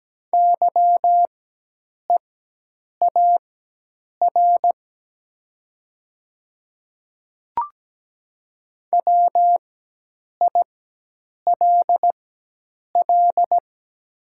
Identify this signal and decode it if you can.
Morse code, heard as a keyed tone, about 17 wpm, spaced as for 8 wpm, 700 Hz. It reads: YEAR WILL
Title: 4x Character Spacing - Top 200 Words - 17wpm